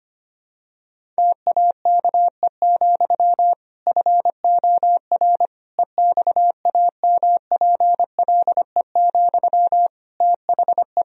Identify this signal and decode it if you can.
Morse code, heard as a keyed tone, about 25 wpm, 700 Hz. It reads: TAKE, FOR EXAMPLE, THE
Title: Sentences from Top 1000 Words - Review Part 1 - 25wpm